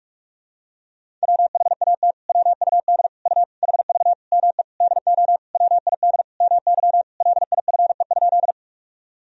Morse code, heard as a keyed tone, about 46 wpm, 700 Hz. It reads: WHAT WUD U HV ME DO WID MY LIFE?
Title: Sentences from Top 300 Words and Abbreviations 45wpm